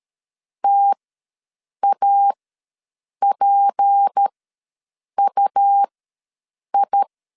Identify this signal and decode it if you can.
Morse code, heard as a keyed tone, about 13 wpm, spaced as for 7 wpm, 800 Hz. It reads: TAPUI
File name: LEZIONI / 8 YJ45 / 8 YJ ws 12wpm 3342